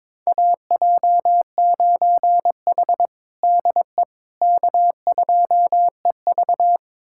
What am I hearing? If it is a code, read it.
Morse code, heard as a keyed tone, about 22 wpm, 700 Hz. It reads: AJ9H DE K2EV
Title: USA Callsign DE Callsign - 4 Characters - 22wpm